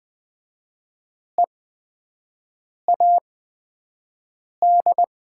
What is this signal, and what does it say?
Morse code, heard as a keyed tone, about 20 wpm, spaced as for 5 wpm, 700 Hz. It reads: EAD
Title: 8x Character Spacing - Top 300 Words - 20wpm